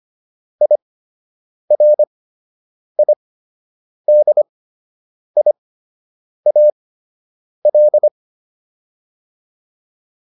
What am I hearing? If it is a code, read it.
Morse code, heard as a keyed tone, about 25 wpm, spaced as for 8 wpm, 600 Hz. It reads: IRIDIAL